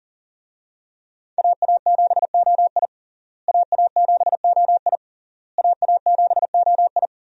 Morse code, heard as a keyed tone, about 40 wpm, 700 Hz. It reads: AA7OI AA7OI AA7OI
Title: USA Call Signs - 5 Characters - Repeated 3x - 40wpm